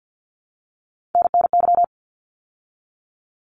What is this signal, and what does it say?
Morse code, heard as a keyed tone, about 50 wpm, 700 Hz. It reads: NNY
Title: ARRL Sections 50wpm